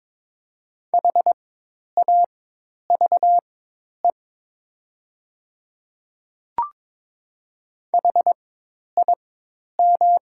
Morse code, heard as a keyed tone, about 22 wpm, spaced as for 10 wpm, 700 Hz. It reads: HAVE HIM
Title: 4x Character Spacing - Top 100 Words - 22wpm